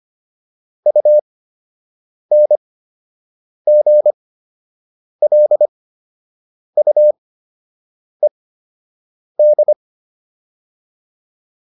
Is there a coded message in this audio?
Yes: Morse code from 0.9 to 9.7 s, heard as a keyed tone, about 25 wpm, spaced as for 7 wpm, 600 Hz. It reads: UNGLUED